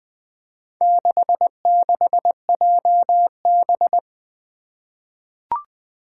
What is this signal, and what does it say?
Morse code, heard as a keyed tone, about 20 wpm, 700 Hz. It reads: 66JB
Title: Four Challenging Letter-Number Combinations 20wpm